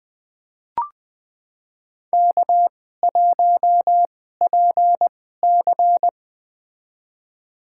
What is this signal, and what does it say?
Morse code, heard as a keyed tone, about 20 wpm, spaced as for 15 wpm, 700 Hz. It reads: K1PC
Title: USA Call Signs - 4 Characters - 2x Character Spacing - 20wpm